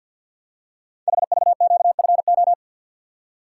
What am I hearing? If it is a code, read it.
Morse code, heard as a keyed tone, about 50 wpm, 700 Hz. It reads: HVQFO
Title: Code Groups - 5 Letters - 50wpm